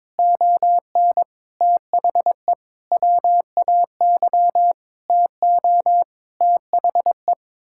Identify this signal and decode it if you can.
Morse code, heard as a keyed tone, about 22 wpm, 700 Hz. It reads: ON THE WAY TO THE
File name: Sets of 9 Words - 22wpm